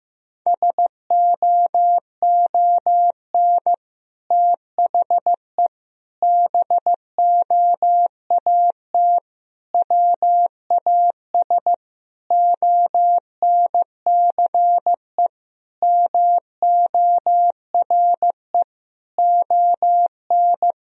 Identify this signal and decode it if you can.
Morse code, heard as a keyed tone, about 16 wpm, 700 Hz. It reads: SOON THE BOAT WAS ONCE MORE ON
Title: Sentences from Top 1000 Words - Review Part 3 - 15wpm